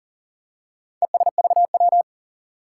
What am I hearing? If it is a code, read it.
Morse code, heard as a keyed tone, about 39 wpm, 700 Hz. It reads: ESVW
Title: Four Challenging Letter-Number Combinations 40wpm